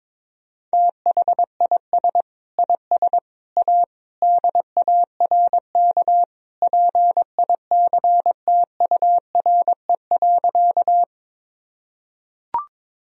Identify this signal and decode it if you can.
Morse code, heard as a keyed tone, about 22 wpm, 700 Hz. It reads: THIS IS A DARK PICTURE.
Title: Sentences from Top 1000 Words - Review Part 1 - 22wpm